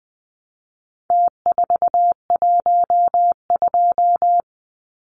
Morse code, heard as a keyed tone, about 20 wpm, 700 Hz. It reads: T412
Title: Numbers - Contest Style - 4-Digit 20wpm